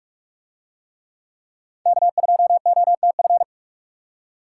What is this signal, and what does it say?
Morse code, heard as a keyed tone, about 44 wpm, 700 Hz. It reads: K2OTF